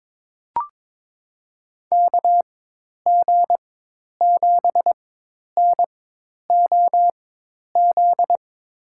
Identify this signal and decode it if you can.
Morse code, heard as a keyed tone, about 22 wpm, spaced as for 10 wpm, 700 Hz. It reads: KG7NOZ
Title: USA Call Signs - 6 Characters - 4x Character Spacing - 22wpm